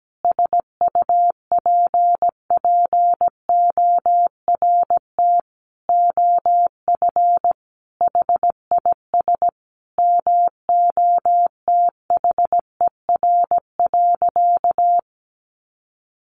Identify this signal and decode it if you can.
Morse code, heard as a keyed tone, about 17 wpm, 700 Hz. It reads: SUPPORT OF HIS MOTHER.
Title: Sentences from Top 800 Words 17wpm